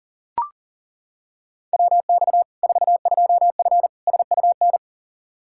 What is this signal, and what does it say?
Morse code, heard as a keyed tone, about 38 wpm, 700 Hz. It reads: WX 42F SUN
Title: K1USN SST QSO Practice 40wpm